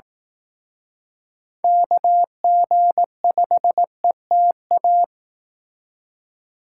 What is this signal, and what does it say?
Morse code, heard as a keyed tone, about 18 wpm, 700 Hz. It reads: KG5ETA
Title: Speed Racing - USA Call Signs 15wpm